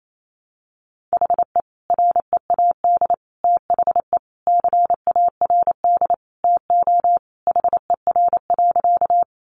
Morse code, heard as a keyed tone, about 28 wpm, 700 Hz. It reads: HE READ THE CARD TO HER.
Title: Sentences from Top 600 Words 28wpm